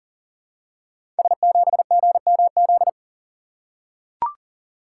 Morse code, heard as a keyed tone, about 38 wpm, 700 Hz. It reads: S7GMZ
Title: Code Groups - 5 Letter:Numbers - 40wpm